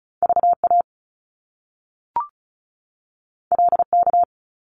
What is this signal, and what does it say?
Morse code, heard as a keyed tone, about 34 wpm, 700 Hz. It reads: V A L K
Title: Top Two Letter Combinations in Words 35wpm